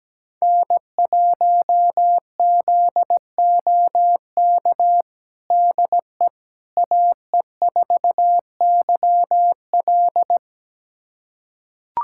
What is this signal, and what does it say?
Morse code, heard as a keyed tone, about 17 wpm, 700 Hz. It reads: N1ZOK DE AE4YL
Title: USA Callsign DE Callsign - 5 Characters - 17wpm